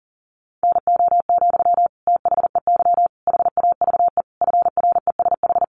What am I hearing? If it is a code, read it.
Morse code, heard as a keyed tone, about 40 wpm, 700 Hz. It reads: NO, THEY HAVE FRESH
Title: Sentences from Top 1300 Words 40wpm